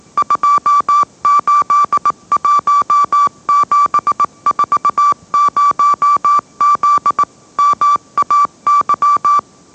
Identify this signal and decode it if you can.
Morse code, heard as a keyed tone, about 18 wpm, 1160 Hz. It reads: 281740Z MAY
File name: mrx / mrx081 20wpm